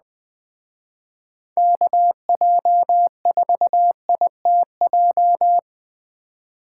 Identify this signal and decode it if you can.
Morse code, heard as a keyed tone, about 20 wpm, 700 Hz. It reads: KJ4ITJ